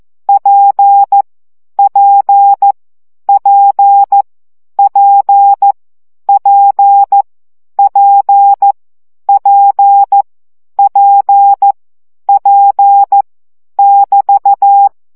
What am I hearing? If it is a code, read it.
Morse code, heard as a keyed tone, about 14 wpm, 800 Hz. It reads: P P P P P P P P P <BT>
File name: CW gyakorlas p 12 baud